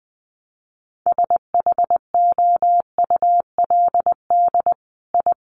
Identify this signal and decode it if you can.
Morse code, heard as a keyed tone, about 20 wpm, 700 Hz. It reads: SHOULD I
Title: Sentences from Top 900 Words 20wpm